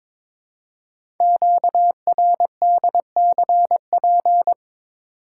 Morse code, heard as a keyed tone, about 22 wpm, 700 Hz. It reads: QRDCP
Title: Code Groups - 5 Letter:Numbers - 22wpm